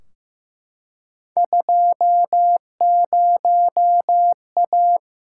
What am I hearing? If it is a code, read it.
Morse code, heard as a keyed tone, about 15 wpm, 700 Hz. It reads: 20A
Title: Field Day Exchanges Encore - 15wpm